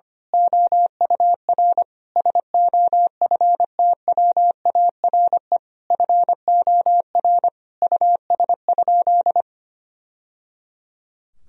Morse code, heard as a keyed tone, about 25 wpm, 700 Hz. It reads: OUR SOFTWARE FOR US?
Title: Sentences from Top 1000 Words - Review Part 3 - 25wpm